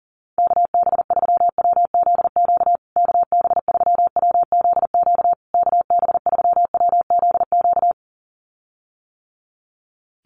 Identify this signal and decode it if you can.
Morse code, heard as a keyed tone, about 40 wpm, 700 Hz. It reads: KB3WZQ KB3WZQ KB3WZQ
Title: USA Call Signs Repeated 3x 40wpm